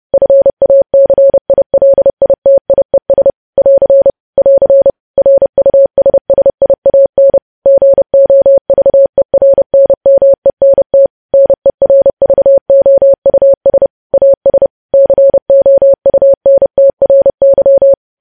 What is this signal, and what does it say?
Morse code, heard as a keyed tone, about 30 wpm, 550 Hz. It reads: FACILITIES <AR> <AR> RUSSIAN GOVERNMENT NERVOUS AS COUNTRY
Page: news headlines in morse code at 30 wpm